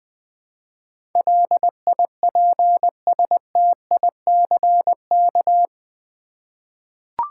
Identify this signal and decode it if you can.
Morse code, heard as a keyed tone, about 20 wpm, 700 Hz. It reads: LIPSTICK
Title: Easy Words with Hints - 20wpm